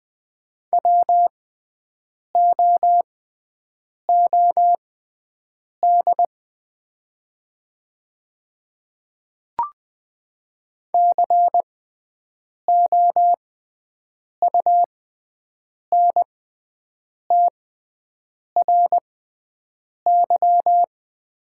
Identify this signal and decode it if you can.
Morse code, heard as a keyed tone, about 20 wpm, spaced as for 7 wpm, 700 Hz. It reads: WOOD COUNTRY
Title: 6x Character Spacing - Top 300 Words - 20wpm